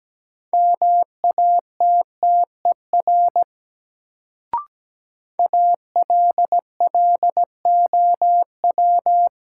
Morse code, heard as a keyed tone, about 17 wpm, 700 Hz. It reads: MATTER ALLOW
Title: Sentences from Top 700 Words - New Word Drill - 17wpm